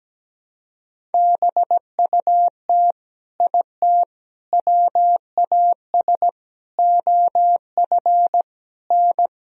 Morse code, heard as a keyed tone, about 17 wpm, 700 Hz. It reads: BUT IT WAS OF N
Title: Sentences from Top 100 Words and Abbreviations 17wpm